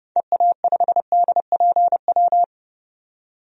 Morse code, heard as a keyed tone, about 29 wpm, 690 Hz. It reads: EA5DPW
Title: DXCC-100 Call Signs - 6 Characters - 30wpm